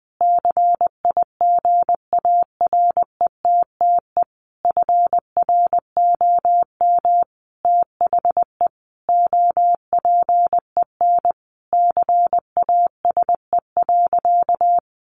Morse code, heard as a keyed tone, about 20 wpm, 700 Hz. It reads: CIGARETTE FROM THE OPEN CASE.
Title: Sentences from Top 1000 Words 20wpm